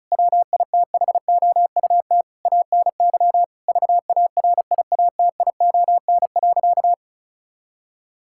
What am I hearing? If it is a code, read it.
Morse code, heard as a keyed tone, about 34 wpm, 700 Hz. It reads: WITHOUT ANY VARIATION.
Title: Sentences from Top 1600 Words 35wpm